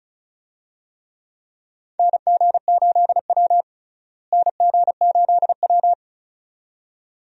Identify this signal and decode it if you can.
Morse code, heard as a keyed tone, about 36 wpm, 700 Hz. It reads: NG8W NG8W
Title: USA Call Signs - 4 Characters - Repeated 2x - 35wpm